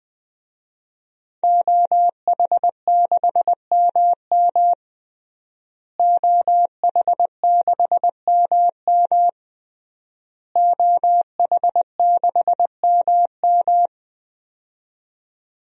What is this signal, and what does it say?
Morse code, heard as a keyed tone, about 20 wpm, 700 Hz. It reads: OH6MM OH6MM OH6MM
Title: DXCC-100 Call Signs - 5 Characters - Repeated 3x - 20wpm